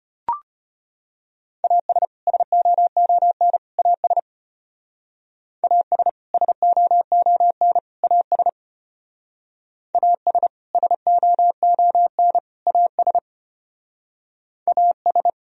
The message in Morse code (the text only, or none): AS SOON AS AS SOON AS AS SOON AS AS